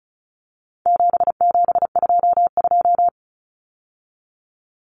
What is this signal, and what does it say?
Morse code, heard as a keyed tone, about 34 wpm, 700 Hz. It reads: 7722
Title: Numbers - 4-Digit 35wpm